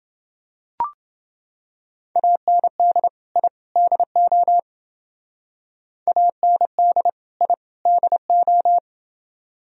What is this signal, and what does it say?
Morse code, heard as a keyed tone, about 29 wpm, 700 Hz. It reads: AND I DO AND I DO